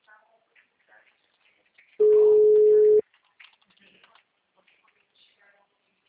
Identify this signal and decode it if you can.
Ringback tone (Europe, 425 Hz), with speech.